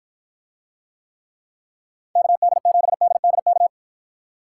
Morse code, heard as a keyed tone, about 48 wpm, 700 Hz. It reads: KD7DDK